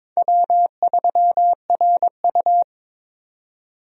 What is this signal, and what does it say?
Morse code, heard as a keyed tone, about 23 wpm, 700 Hz. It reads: W3RU
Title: USA Call Signs - 4 Characters - 22wpm